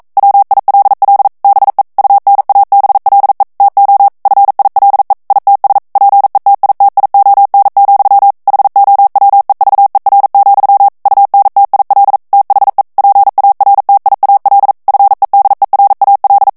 Morse code, heard as a keyed tone, about 42 wpm, 800 Hz. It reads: WILL BE UNABLE TO FILE ITS PETITION, HOWEVER, UNTIL THE PARTIAL FEDERAL
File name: 40 mots minute